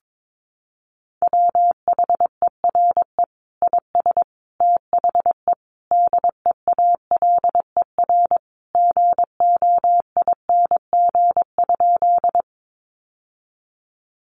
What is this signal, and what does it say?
Morse code, heard as a keyed tone, about 22 wpm, 700 Hz. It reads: WHERE IS THE DEALER GOING?
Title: Sentences from Top 1100 Words 22wpm